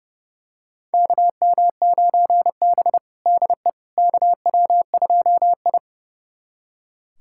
Morse code, heard as a keyed tone, about 32 wpm, 700 Hz. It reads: KM9B DE KW2I
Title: USA Callsign DE Callsign - 4 Characters - 30wpm